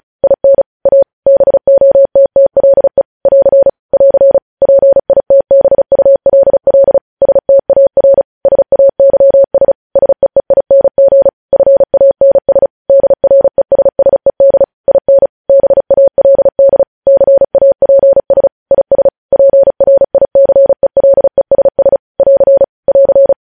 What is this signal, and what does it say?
Morse code, heard as a keyed tone, about 36 wpm, 550 Hz. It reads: IN A BOTTLE <AR> <AR> PITBULL STAR SAYS SEEING FANS DRESSED IN BALD CAPS IS PRICELESS <AR> <AR>
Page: news headlines in morse code at 35 wpm